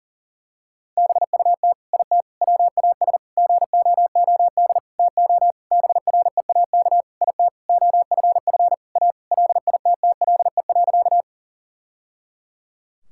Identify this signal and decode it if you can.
Morse code, heard as a keyed tone, about 37 wpm, 700 Hz. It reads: BUT IT WAS GOOD TO BREAK IT OFF A LITTLE.
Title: Sentences from Top 1000 Words - Review Part 6 - 40wpm